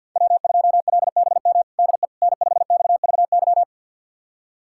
Morse code, heard as a keyed tone, about 52 wpm, 700 Hz. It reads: W2LBM DE N5XVY
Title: USA Callsign DE Callsign 50wpm